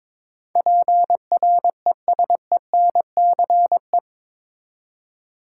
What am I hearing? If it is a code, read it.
Morse code, heard as a keyed tone, about 22 wpm, 700 Hz. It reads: PRESENCE
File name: practice / 22wpm / Eight Letter Words - 22wpm